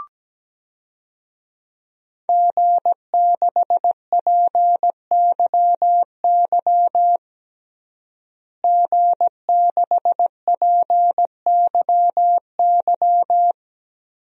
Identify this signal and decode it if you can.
Morse code, heard as a keyed tone, about 17 wpm, 700 Hz. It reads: G6PYY G6PYY
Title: DXCC-100 Call Signs - 5 Characters - Repeated 3x - 17wpm